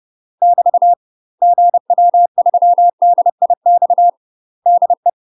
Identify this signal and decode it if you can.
Morse code, heard as a keyed tone, about 30 wpm, 700 Hz. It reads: <BT> GW3DIX DE
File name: UK MASTER QSOs TRAINING / 30WPM / UK MASTER QSOs 0018